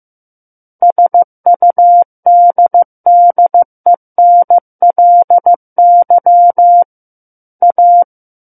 Morse code, heard as a keyed tone, about 15 wpm, 700 Hz. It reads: SUDDENLY A
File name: Chapter 6 - 15 WPM